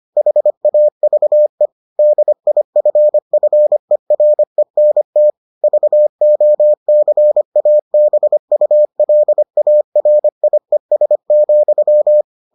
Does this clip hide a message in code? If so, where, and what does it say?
Yes: Morse code from 0.2 to 12.2 s, heard as a keyed tone, about 25 wpm, 600 Hz. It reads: HAVE DIFFERENT VOCABULARIES,